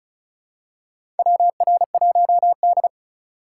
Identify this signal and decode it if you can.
Morse code, heard as a keyed tone, about 36 wpm, 700 Hz. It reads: WR1D
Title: USA Call Signs - 4 Characters - 4x Character Spacing - 35wpm